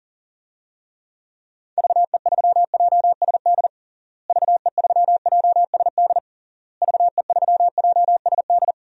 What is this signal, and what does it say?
Morse code, heard as a keyed tone, about 37 wpm, 700 Hz. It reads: VE3JSD VE3JSD VE3JSD
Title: DXCC-100 Call Signs - 6 Characters - Repeated 3x - 40wpm